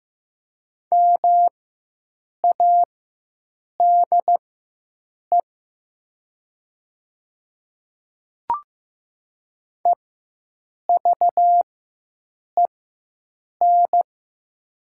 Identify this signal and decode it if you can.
Morse code, heard as a keyed tone, about 15 wpm, spaced as for 7 wpm, 700 Hz. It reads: MADE EVEN
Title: 4x Character Spacing - Top 300 Words - 15wpm